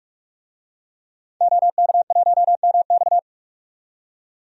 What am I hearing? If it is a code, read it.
Morse code, heard as a keyed tone, about 46 wpm, 700 Hz. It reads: OK1MX
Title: DXCC-100 Call Signs - 5 Characters - 45wpm